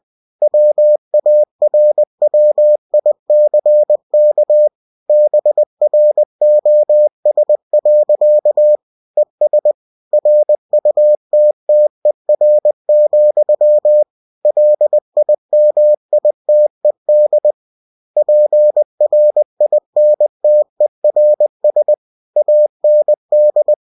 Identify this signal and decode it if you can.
Morse code, heard as a keyed tone, about 20 wpm, 600 Hz. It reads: WARWICK BROS. ES RUTTER, LIMITED PRINTERS AND
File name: cw-books / winnie-the-pooh / winnie-20wpm-page-0111